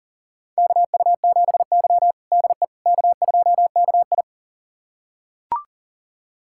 Morse code, heard as a keyed tone, about 40 wpm, 700 Hz. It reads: KU7Y DE K2KI
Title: USA Callsign DE Callsign - 4 Characters - 40wpm